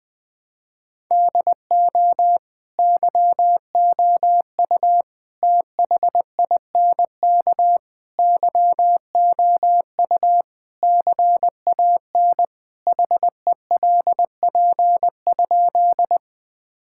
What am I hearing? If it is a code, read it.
Morse code, heard as a keyed tone, about 20 wpm, 700 Hz. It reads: DO YOU THINK YOU CAN HELP?
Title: Sentences with 4-Word Phrases - 20wpm